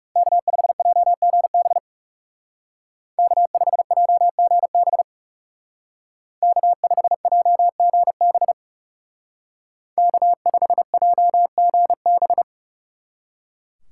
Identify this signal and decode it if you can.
Morse code, heard as a keyed tone, about 37 wpm, 700 Hz. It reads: K5JGB K5JGB K5JGB K5JGB
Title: Speed Racing - USA 1x3 Call Signs 30wpm